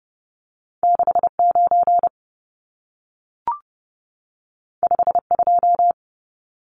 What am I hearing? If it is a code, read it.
Morse code, heard as a keyed tone, about 30 wpm, 700 Hz. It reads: 69 52
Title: Numbers - 2-Digit 30wpm